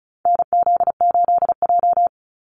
Morse code, heard as a keyed tone, about 34 wpm, 700 Hz. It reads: NZ8J
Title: Call Signs - 35wpm